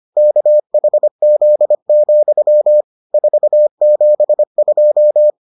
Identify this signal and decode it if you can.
Morse code, heard as a keyed tone, about 25 wpm, 600 Hz. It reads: KHZ, 472